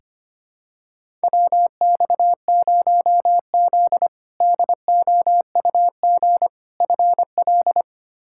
Morse code, heard as a keyed tone, about 25 wpm, 700 Hz. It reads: WX0Z DOUG FL
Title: K1USN SST QSO Practice - 25wpm